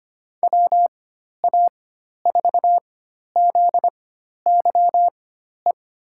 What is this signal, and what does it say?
Morse code, heard as a keyed tone, about 25 wpm, spaced as for 12 wpm, 700 Hz. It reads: WA4ZYE
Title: USA Call Signs - 4x Character Spacing - 25wpm